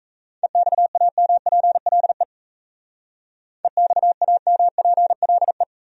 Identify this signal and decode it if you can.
Morse code, heard as a keyed tone, about 34 wpm, 700 Hz. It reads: EXAMPLE EXAMPLE
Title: Speed-Racing Top 300 Words 28wpm